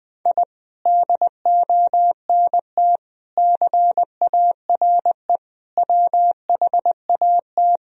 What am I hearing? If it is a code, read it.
Morse code, heard as a keyed tone, about 20 wpm, 700 Hz. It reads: I DONT CARE WHAT
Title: Sentences from Top 1000 Words - Review Part 1 - 20wpm